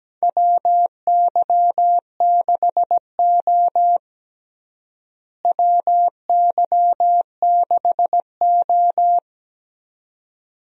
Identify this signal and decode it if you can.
Morse code, heard as a keyed tone, about 17 wpm, 700 Hz. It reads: WY6O WY6O
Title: USA Call Signs - 4 Characters - Repeated 2x - 17wpm